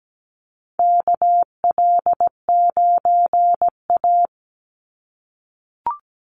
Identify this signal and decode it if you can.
Morse code, heard as a keyed tone, about 17 wpm, 700 Hz. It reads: KL9A
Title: Call Signs - 17wpm